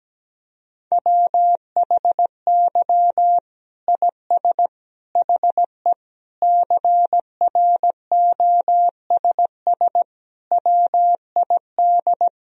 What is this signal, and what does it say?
Morse code, heard as a keyed tone, about 17 wpm, 700 Hz. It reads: WHY IS HE CROSS WID
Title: Sentences from Top 300 Words and Abbreviations 17wpm